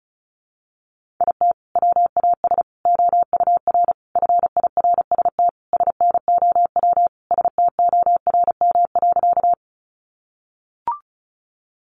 Morse code, heard as a keyed tone, about 34 wpm, 700 Hz. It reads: IT WAS OUR FIRST SNOW STORM.
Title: Sentences from Top 1500 Words 35wpm